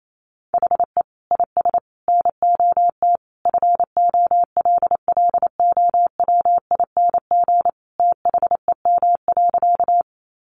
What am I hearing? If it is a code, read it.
Morse code, heard as a keyed tone, about 28 wpm, 700 Hz. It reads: HE IS NOT FOLLOWING THEM.
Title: Sentences from Top 700 Words 28wpm